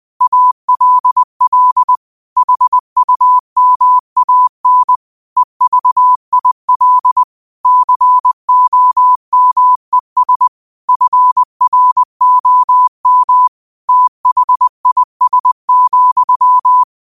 Morse code, heard as a keyed tone, about 20 wpm, 1000 Hz. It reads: ALL HUMAN EVIL COMES FROM THIS,
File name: qotdfiles / 20wpm